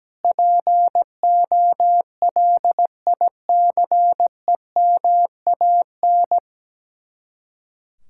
Morse code, heard as a keyed tone, about 17 wpm, 700 Hz. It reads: POLICEMAN
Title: Speed-Racing Common QSO Phrases - 17wpm